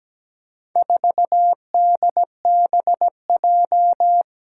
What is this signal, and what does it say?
Morse code, heard as a keyed tone, about 17 wpm, 700 Hz. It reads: 4DBJ
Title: Four Challenging Letter-Number Combinations 17wpm